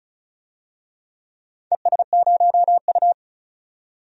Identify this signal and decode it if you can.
Morse code, heard as a keyed tone, about 36 wpm, 700 Hz. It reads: ES0U
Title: DXCC-100 Call Signs - 4 Characters - 35wpm